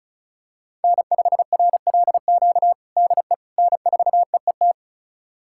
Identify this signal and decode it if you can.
Morse code, heard as a keyed tone, about 34 wpm, 700 Hz. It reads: N5RLQ DE N4EET
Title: USA Callsign DE Callsign - 5 Characters - 35wpm